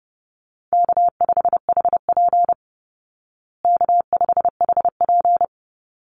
Morse code, heard as a keyed tone, about 30 wpm, 700 Hz. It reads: K5HP K5HP
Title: Call Signs Repeated 3x - 30wpm